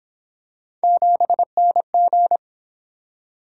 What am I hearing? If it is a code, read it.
Morse code, heard as a keyed tone, about 26 wpm, 700 Hz. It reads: 7NG
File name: Speed-Racing - Three Letter-Number Combinations - 22wpm